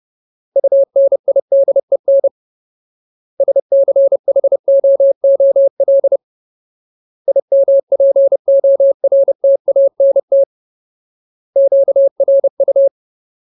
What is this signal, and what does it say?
Morse code, heard as a keyed tone, about 30 wpm, 550 Hz. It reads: UNIDEN SCHOOL IMPORTANT QRU
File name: Ultimate Word List / 30WPM Ultimate